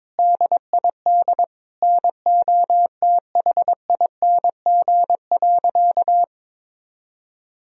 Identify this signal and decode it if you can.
Morse code, heard as a keyed tone, about 22 wpm, 700 Hz. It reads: DID NOTHING.
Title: Sentences from Top 1000 Words - Review Part 3 - 22wpm